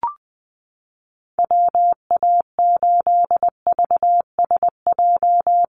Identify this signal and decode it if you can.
Morse code, heard as a keyed tone, about 20 wpm, 700 Hz. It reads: WA8VSJ
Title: Call Signs - 20wpm